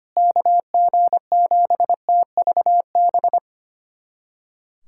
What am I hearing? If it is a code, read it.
Morse code, heard as a keyed tone, about 25 wpm, 700 Hz. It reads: KG7TVB
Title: USA Callsign DE Callsign 25wpm